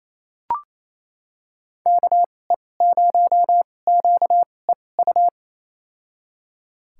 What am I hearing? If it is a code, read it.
Morse code, heard as a keyed tone, about 28 wpm, spaced as for 20 wpm, 700 Hz. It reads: KE0QEU
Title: USA Call Signs - 6 Characters - 2x Character Spacing - 28wpm